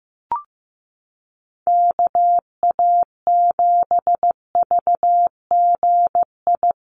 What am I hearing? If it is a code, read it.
Morse code, heard as a keyed tone, about 15 wpm, 700 Hz. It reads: KA7VGI
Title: USA Call Signs Repeated 2x 15wpm